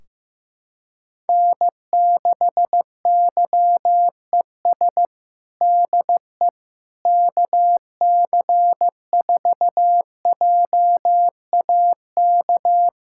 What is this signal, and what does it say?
Morse code, heard as a keyed tone, about 16 wpm, 700 Hz. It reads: N6YES DE KC4JAK
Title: K1USN SST QSO Practice 15wpm